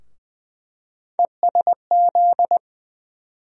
Morse code, heard as a keyed tone, about 20 wpm, 700 Hz. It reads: ESZ
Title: USA Call Sign Postfixes - 3 Letters 20wpm